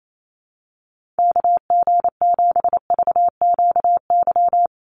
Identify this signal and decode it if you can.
Morse code, heard as a keyed tone, about 28 wpm, 700 Hz. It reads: KG7VQY
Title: USA Call Signs Repeated 3x 28wpm